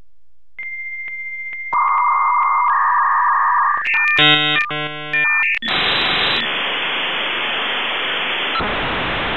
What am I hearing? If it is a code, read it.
Dial-up modem handshake.